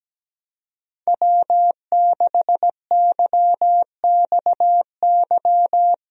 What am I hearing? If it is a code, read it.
Morse code, heard as a keyed tone, about 17 wpm, 700 Hz. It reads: W6YXY
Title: USA Call Signs - 5 Characters - 17wpm